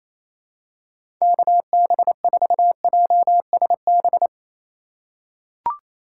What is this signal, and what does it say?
Morse code, heard as a keyed tone, about 28 wpm, 700 Hz. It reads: KB4JSB
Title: USA Call Signs - 6 Characters - 2x Character Spacing - 28wpm